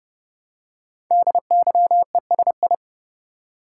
Morse code, heard as a keyed tone, about 30 wpm, 690 Hz. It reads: DYESI
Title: Code Groups - 5 Letters - 30wpm